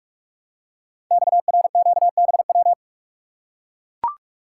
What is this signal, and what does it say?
Morse code, heard as a keyed tone, about 44 wpm, 700 Hz. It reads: XRQBW
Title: Code Groups - 5 Letters - 45wpm